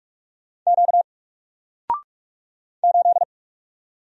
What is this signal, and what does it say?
Morse code, heard as a keyed tone, about 46 wpm, 700 Hz. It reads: Q 8
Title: Single Letter-Number Alt 45wpm